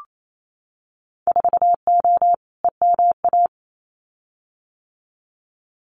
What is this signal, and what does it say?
Morse code, heard as a keyed tone, about 28 wpm, 700 Hz. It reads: 4O EMA
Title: Winter Field Day Exchanges Encore 28wpm